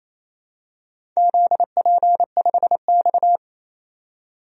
Morse code, heard as a keyed tone, about 28 wpm, 700 Hz. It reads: ZP5X
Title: DXCC-100 Call Signs - 4 Characters - 28wpm